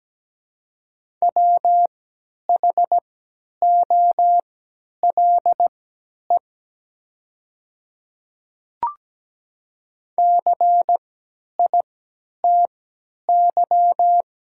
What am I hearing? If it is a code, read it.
Morse code, heard as a keyed tone, about 17 wpm, spaced as for 10 wpm, 700 Hz. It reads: WHOLE CITY
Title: 3x Character Spacing - Top 500 Words - 17wpm